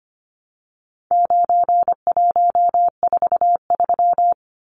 Morse code, heard as a keyed tone, about 25 wpm, 700 Hz. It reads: 9143
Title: Numbers - 4-Digit 25wpm